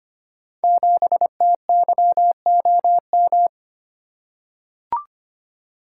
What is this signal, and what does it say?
Morse code, heard as a keyed tone, about 25 wpm, 700 Hz. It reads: 7TYOM
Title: Code Groups - 5 Letter:Numbers - 25wpm